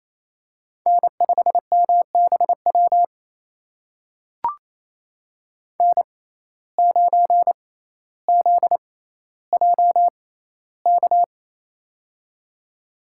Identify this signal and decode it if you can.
Morse code, heard as a keyed tone, about 28 wpm, spaced as for 14 wpm, 700 Hz. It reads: N5MBW N9ZJK
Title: USA Call Signs - 5 Characters - 6x Character Spacing - 28wpm